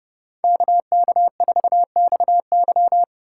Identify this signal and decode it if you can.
Morse code, heard as a keyed tone, about 30 wpm, 700 Hz. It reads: KK4XY
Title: USA Call Signs - 5 Characters - 4x Character Spacing - 30wpm